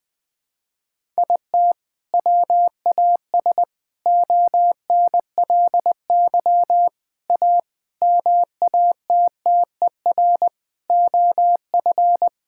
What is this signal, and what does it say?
Morse code, heard as a keyed tone, about 20 wpm, 700 Hz. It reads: IT WAS ONLY A MATTER OF
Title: Sentences with 8-Word Phrases - 20wpm